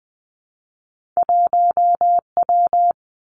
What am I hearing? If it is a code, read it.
Morse code, heard as a keyed tone, about 20 wpm, 700 Hz. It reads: 1 W
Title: Two Challenging Letter-Number Combinations 20wpm